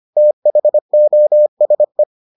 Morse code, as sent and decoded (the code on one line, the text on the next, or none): - .... --- ... .
THOSE